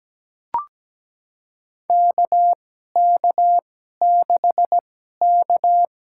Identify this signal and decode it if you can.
Morse code, heard as a keyed tone, about 17 wpm, spaced as for 12 wpm, 700 Hz. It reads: KK6K
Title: USA Call Signs - 6 Characters - 2x Character Spacing - 17wpm